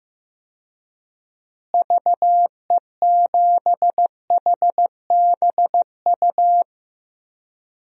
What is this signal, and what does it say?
Morse code, heard as a keyed tone, about 15 wpm, 700 Hz. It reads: VE7HBU
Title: DXCC-100 Call Signs - 6 Characters - Repeated 2x - 15wpm